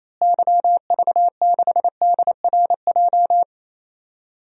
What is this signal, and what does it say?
Morse code, heard as a keyed tone, about 28 wpm, 700 Hz. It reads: YV6DRJ
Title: DXCC-100 Call Signs - 6 Characters - Repeated 2x - 28wpm